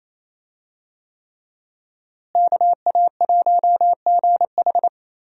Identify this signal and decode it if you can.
Morse code, heard as a keyed tone, about 28 wpm, 700 Hz. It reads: KA1GH
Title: USA Call Signs - 5 Characters - Repeated 3x - 28wpm